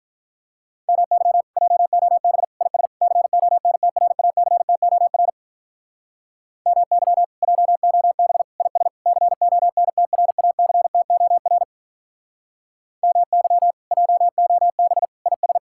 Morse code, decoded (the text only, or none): MY JOB IS CONTRACTOR MY JOB IS CONTRACTOR MY JOB IS